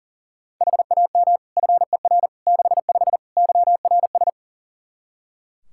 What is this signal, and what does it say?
Morse code, heard as a keyed tone, about 37 wpm, 700 Hz. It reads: HAM FER 65 YRS
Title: Practice QSO - Exchange 3 - More Info - 40wpm